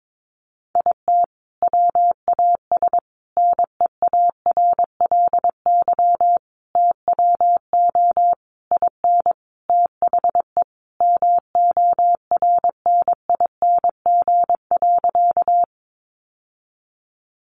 Morse code, heard as a keyed tone, about 22 wpm, 700 Hz. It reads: IT WAS NEARLY TWO IN THE MORNING.
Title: Sentences from Top 1300 Words 22wpm